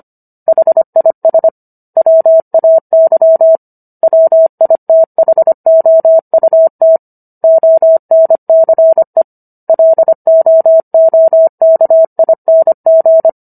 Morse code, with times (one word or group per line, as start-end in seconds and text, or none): 0.5-1.5 HIS
2.0-3.6 WAY
4.0-7.0 WITHOUT
7.4-9.2 ONCE
9.7-13.3 LOOKING